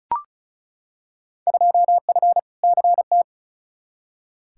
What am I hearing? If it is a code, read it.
Morse code, heard as a keyed tone, about 34 wpm, 700 Hz. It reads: 2F CT
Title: Field Day Exchanges Encore - 35wpm